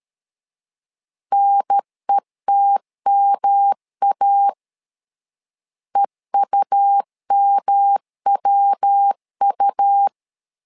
Morse code, heard as a keyed tone, about 13 wpm, 800 Hz. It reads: NETMA EUMWU
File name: LEZIONI / 3 MWRO / 3 MW 12wpm 1633